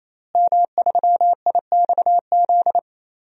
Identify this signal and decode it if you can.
Morse code, heard as a keyed tone, about 28 wpm, 700 Hz. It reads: M3IXZ
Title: DXCC-100 Call Signs - 5 Characters - Repeated 3x - 28wpm